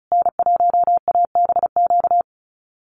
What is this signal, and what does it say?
Morse code, heard as a keyed tone, about 34 wpm, 700 Hz. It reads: N1ABQ
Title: USA Call Signs Repeated 3x 35wpm